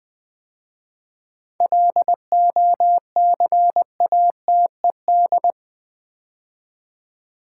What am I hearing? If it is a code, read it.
Morse code, heard as a keyed tone, about 20 wpm, 690 Hz. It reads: LOCATED
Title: 7 Letter Words - First 3 Letters Given - 20wpm